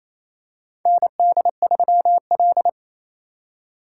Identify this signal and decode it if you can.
Morse code, heard as a keyed tone, about 28 wpm, 700 Hz. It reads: ND3L
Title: K1USN SST QSO Practice - 28wpm